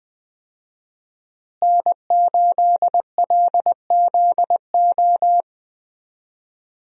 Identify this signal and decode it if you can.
Morse code, heard as a keyed tone, about 20 wpm, 700 Hz. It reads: N8LZO
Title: USA Call Signs - 5 Characters - Repeated 3x - 20wpm